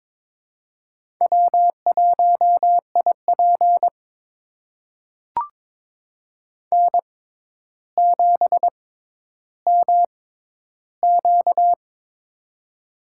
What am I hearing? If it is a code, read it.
Morse code, heard as a keyed tone, about 21 wpm, spaced as for 11 wpm, 700 Hz. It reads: W1IP N7MQ
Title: USA Call Signs - 4 Characters - 6x Character Spacing - 22wpm